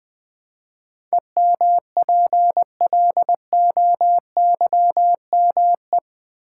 Morse code, heard as a keyed tone, about 20 wpm, 700 Hz. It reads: EMPLOYME
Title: Build A Word - 4 Letters At A Time - 10-Letter Words - 20wpm